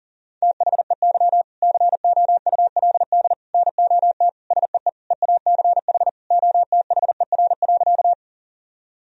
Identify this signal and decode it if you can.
Morse code, heard as a keyed tone, about 40 wpm, 700 Hz. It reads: THEY COULD NOT SEE EACH OTHER.
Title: Sentences from Top 1000 Words - Review Part 1 - 40wpm